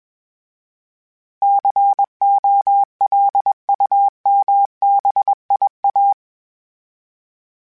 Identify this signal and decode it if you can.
Morse code, heard as a keyed tone, about 21 wpm, 800 Hz. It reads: COLUMBIA
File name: fromSMs / FromSMs mp3s / Cities@20